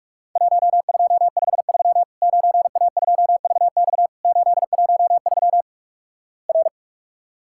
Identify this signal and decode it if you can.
Morse code, heard as a keyed tone, about 46 wpm, 700 Hz. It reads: 1253 9A2VX 813 R
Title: Worked All Europe DX Contest - QTC Exchange - 45wpm